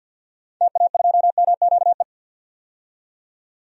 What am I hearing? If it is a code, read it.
Morse code, heard as a keyed tone, about 50 wpm, 700 Hz. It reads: TA2MQE
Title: DXCC-100 Call Signs - 6 Characters - 50wpm